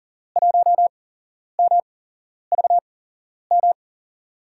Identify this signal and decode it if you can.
Morse code, heard as a keyed tone, about 40 wpm, spaced as for 11 wpm, 700 Hz. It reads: 1MVM
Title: USA Call Signs - 8x Character Spacing - 40wpm